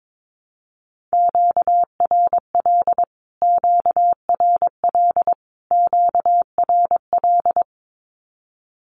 Morse code, heard as a keyed tone, about 22 wpm, 700 Hz. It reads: QRL QRL QRL
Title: Q-Codes Repeated 3x 22wpm